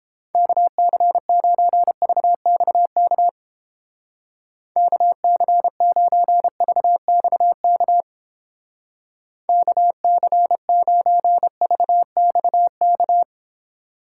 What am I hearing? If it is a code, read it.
Morse code, heard as a keyed tone, about 29 wpm, 700 Hz. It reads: KC9VXK KC9VXK KC9VXK